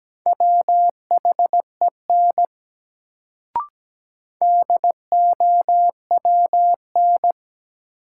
Morse code, heard as a keyed tone, about 17 wpm, 700 Hz. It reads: WHEN DOWN
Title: Sets of 2 Words - New Word Drill - 17wpm